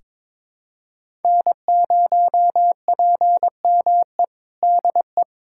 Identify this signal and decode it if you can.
Morse code, heard as a keyed tone, about 22 wpm, 700 Hz. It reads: N0PME DE
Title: USA Callsign DE Callsign - 5 Characters - 22wpm